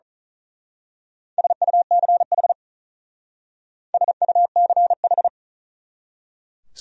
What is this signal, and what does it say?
Morse code, heard as a keyed tone, about 39 wpm, 700 Hz. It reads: SUCH SUCH
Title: Speed-Racing Top 200 Words 35wpm